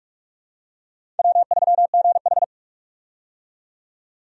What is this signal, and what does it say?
Morse code, heard as a keyed tone, about 45 wpm, 700 Hz. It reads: W3GH
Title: USA Call Signs - 4 Characters - 45wpm